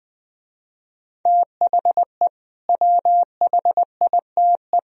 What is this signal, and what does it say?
Morse code, heard as a keyed tone, about 20 wpm, 700 Hz. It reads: THE WHITE
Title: Speed-Racing - Sets of 3 Words 20wpm